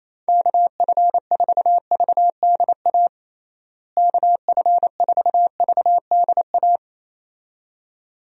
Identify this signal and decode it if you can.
Morse code, heard as a keyed tone, about 27 wpm, 700 Hz. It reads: KF4VDA KF4VDA
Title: USA Call Signs - 6 Characters - Repeated 2x - 28wpm